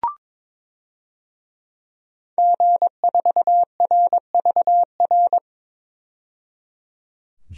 Morse code, heard as a keyed tone, about 22 wpm, 700 Hz. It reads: G4RVR
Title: DXCC-100 Call Signs - 5 Characters - 22wpm